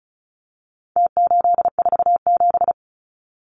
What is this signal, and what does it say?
Morse code, heard as a keyed tone, about 34 wpm, 700 Hz. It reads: T847
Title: Numbers - Contest Style - 4-Digit 35wpm